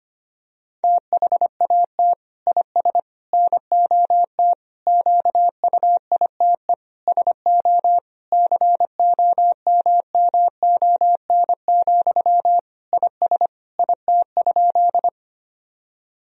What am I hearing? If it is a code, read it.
Morse code, heard as a keyed tone, about 25 wpm, 700 Hz. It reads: THAT IS NOT QUITE SO COMMON, IS IT?
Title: Sentences from Top 1000 Words - Review Part 3 - 25wpm